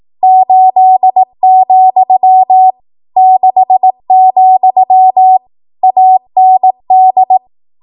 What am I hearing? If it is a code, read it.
Morse code, heard as a keyed tone, about 18 wpm, 750 Hz. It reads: 8, 6, AND
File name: Morse / 18WPM